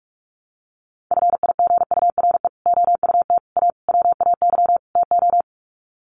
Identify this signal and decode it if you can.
Morse code, heard as a keyed tone, about 44 wpm, 700 Hz. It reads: FIGURE OUT A WAY TO
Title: Sets of 5 Words Encore2 45wpm